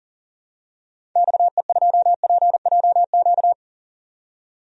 Morse code, heard as a keyed tone, about 39 wpm, 700 Hz. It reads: XE2PJQ
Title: DXCC-100 Call Signs - 6 Characters - 40wpm